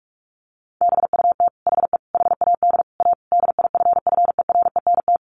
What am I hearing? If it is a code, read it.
Morse code, heard as a keyed tone, about 44 wpm, 700 Hz. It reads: BUT HE HAD A DIFFERENT
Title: Sentences from Top 1200 Words 45wpm